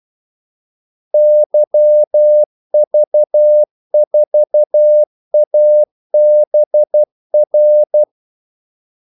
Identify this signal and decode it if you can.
Morse code, heard as a keyed tone, about 12 wpm, 600 Hz. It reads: YV4ABR
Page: 2 ekstra ordmellemrum mellem kaldesignalerne